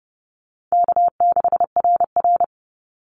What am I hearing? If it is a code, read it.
Morse code, heard as a keyed tone, about 30 wpm, 700 Hz. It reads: K6RR
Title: Call Signs Repeated 3x - 30wpm